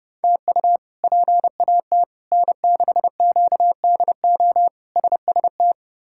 Morse code, heard as a keyed tone, about 30 wpm, 700 Hz. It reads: TU PAT N6QDO SST
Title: K1USN SST QSO Practice - 30wpm